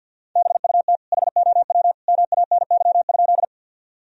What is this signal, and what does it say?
Morse code, heard as a keyed tone, about 53 wpm, 700 Hz. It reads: BUT HOW MANY?
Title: Sentences from Top 1000 Words - Review Part 1 - 50wpm